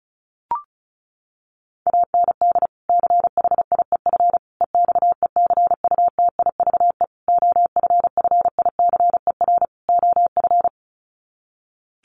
Morse code, heard as a keyed tone, about 34 wpm, 700 Hz. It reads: AND CHIEF EXECUTIVE OFFICER OF
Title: Sets of 5 Words Encore2 35wpm